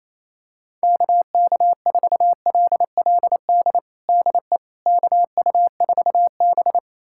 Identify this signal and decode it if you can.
Morse code, heard as a keyed tone, about 27 wpm, 700 Hz. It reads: KK4LLD DE KU4B
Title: USA Callsign DE Callsign 28wpm